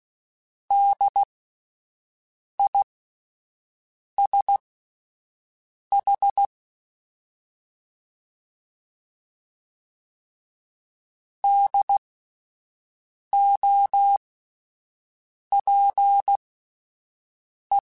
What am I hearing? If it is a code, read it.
Morse code, heard as a keyed tone, about 16 wpm, spaced as for 5 wpm, 800 Hz. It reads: DISH DOPE